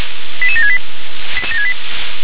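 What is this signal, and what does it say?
Five-tone selective call (ZVEI), heard as quick runs of five tones, tones of about 70 ms. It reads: 88768
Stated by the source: ZVEI five-tone selective call received by SDR.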